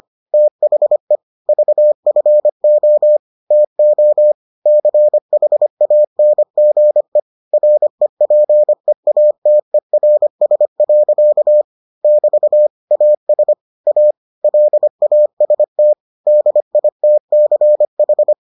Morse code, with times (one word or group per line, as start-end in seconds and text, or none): 0.3-1.2 THE
1.5-3.2 VFO
3.5-4.3 TO
4.7-7.2 CHANGE
7.5-11.6 REPEATERS.
12.0-12.7 <BT>
12.9-13.5 AS
13.9-14.1 A
14.4-15.9 LAST
16.3-18.3 DITCH